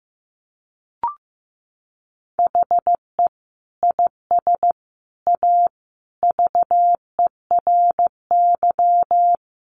Morse code, heard as a keyed tone, about 15 wpm, 700 Hz. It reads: HE IS A VERY
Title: Sentences from Top 1200 Words 15wpm